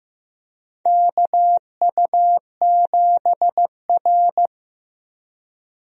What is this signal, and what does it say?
Morse code, heard as a keyed tone, about 15 wpm, 700 Hz. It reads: KU7R
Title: USA Call Signs - 4 Characters - Repeated 2x - 15wpm